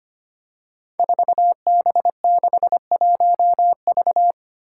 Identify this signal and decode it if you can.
Morse code, heard as a keyed tone, about 25 wpm, 700 Hz. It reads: 4B61V
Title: Five Challenging Letter-Number Combinations 25wpm